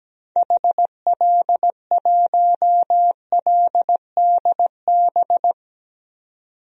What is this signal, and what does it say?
Morse code, heard as a keyed tone, about 17 wpm, 700 Hz. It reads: HL1LDB
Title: DXCC-100 Call Signs - 6 Characters - 17wpm